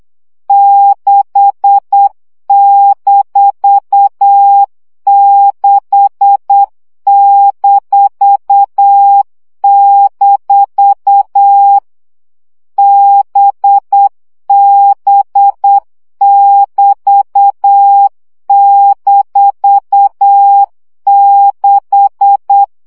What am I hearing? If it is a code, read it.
Morse code, heard as a keyed tone, about 8 wpm, 800 Hz. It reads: 6-6-- BB<BT>-6
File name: CW gyakorlas b 6 egyenlo minusz 7 baud